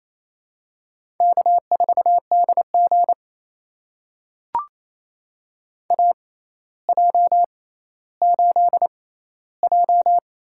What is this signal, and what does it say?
Morse code, heard as a keyed tone, about 28 wpm, spaced as for 14 wpm, 700 Hz. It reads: K4DG AJ8J
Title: USA Call Signs - 4 Characters - 6x Character Spacing - 28wpm